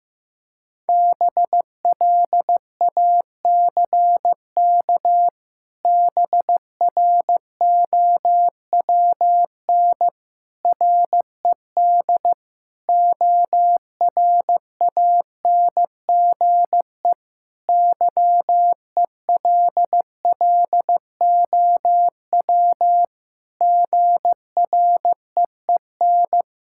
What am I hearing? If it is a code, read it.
Morse code, heard as a keyed tone, about 15 wpm, 700 Hz. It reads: BLACK BROWN RED ORANGE YELLOW GREEN
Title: Pangram Challenge v2 - 15wpm